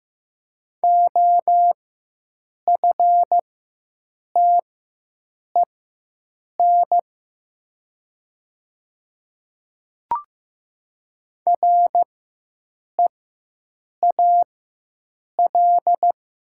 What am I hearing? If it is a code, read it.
Morse code, heard as a keyed tone, about 15 wpm, spaced as for 7 wpm, 700 Hz. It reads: OFTEN REAL
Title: 4x Character Spacing - Top 300 Words - 15wpm